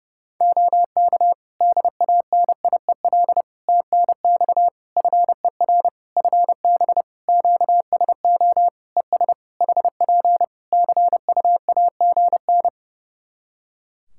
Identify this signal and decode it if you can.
Morse code, heard as a keyed tone, about 28 wpm, 700 Hz. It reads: OK DANIEL TNX FER FB QSO ES HP CUAGN
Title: Practice QSO - Exchange 4 - Ending - 30wpm